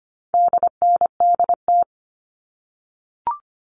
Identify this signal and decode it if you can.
Morse code, heard as a keyed tone, about 25 wpm, 700 Hz. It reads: DNDT
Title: All Cut Numbers - 4-Digit 25wpm